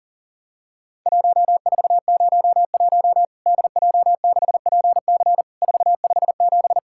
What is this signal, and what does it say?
Morse code, heard as a keyed tone, about 40 wpm, 700 Hz. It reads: 1401 DJ6PC 457
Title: Worked All Europe DX Contest - QTC Exchange - 40wpm